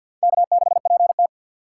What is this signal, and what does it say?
Morse code, heard as a keyed tone, about 61 wpm, 700 Hz. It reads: K6PT
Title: Speed Racing - USA 1x2 Call Signs 50wpm